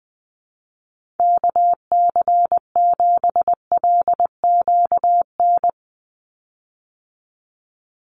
Morse code, heard as a keyed tone, about 20 wpm, 700 Hz. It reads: KC7LQN
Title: USA Call Signs 20wpm